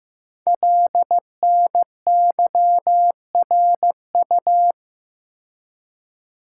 Morse code, heard as a keyed tone, about 15 wpm, 700 Hz. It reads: LNYRU
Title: Code Groups - 5 Letter:Numbers - 15wpm